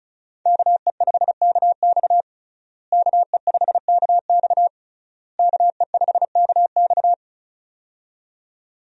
Morse code, heard as a keyed tone, about 36 wpm, 700 Hz. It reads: KE5KX KE5KX KE5KX
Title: USA Call Signs - 5 Characters - Repeated 3x - 35wpm